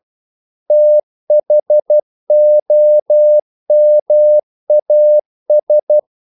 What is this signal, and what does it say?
Morse code, heard as a keyed tone, about 12 wpm, 600 Hz. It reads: THOMAS
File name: CW / Practice / EnglishQuotations0112